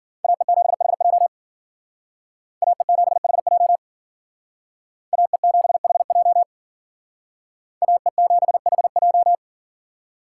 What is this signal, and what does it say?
Morse code, heard as a keyed tone, about 51 wpm, 700 Hz. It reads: AE7HJ AE7HJ AE7HJ AE7HJ